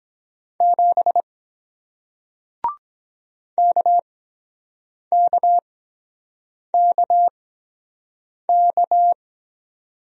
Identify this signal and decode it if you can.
Morse code, heard as a keyed tone, about 23 wpm, spaced as for 7 wpm, 700 Hz. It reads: 7 KKKK